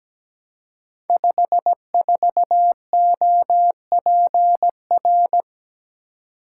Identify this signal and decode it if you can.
Morse code, heard as a keyed tone, about 17 wpm, 700 Hz. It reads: 54OPR
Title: Code Groups - 5 Letter:Numbers - 17wpm